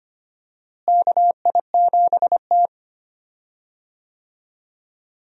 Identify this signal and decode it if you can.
Morse code, heard as a keyed tone, about 25 wpm, 700 Hz. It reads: KI7T
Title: USA Call Signs - 4 Characters - Repeated 3x - 25wpm